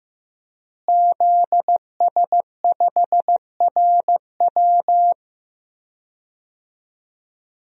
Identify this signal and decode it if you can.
Morse code, heard as a keyed tone, about 15 wpm, 700 Hz. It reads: ZS5RW
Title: DXCC-100 Call Signs - 5 Characters - Repeated 3x - 15wpm